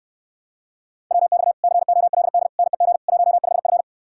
Morse code, heard as a keyed tone, about 69 wpm, 700 Hz. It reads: WX COLD NEG 15F